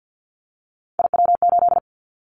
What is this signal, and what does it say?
Morse code, heard as a keyed tone, about 50 wpm, 700 Hz. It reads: IW8
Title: Callsign Prefixes for Top 25 DXCC - 50wpm